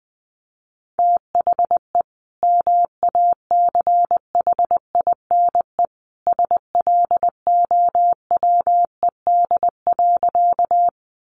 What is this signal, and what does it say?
Morse code, heard as a keyed tone, about 20 wpm, 700 Hz. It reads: THE MACHINE SLOWED.
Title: A Sound Of Thunder 20wpm